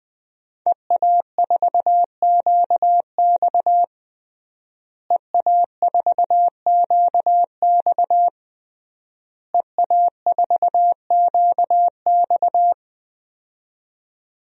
Morse code, heard as a keyed tone, about 20 wpm, 700 Hz. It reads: EA4QX EA4QX EA4QX
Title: DXCC-100 Call Signs - 5 Characters - Repeated 3x - 20wpm